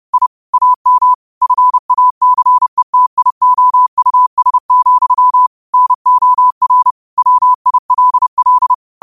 Morse code, heard as a keyed tone, about 30 wpm, 1000 Hz. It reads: I AM FACETIOUS, NOR WILL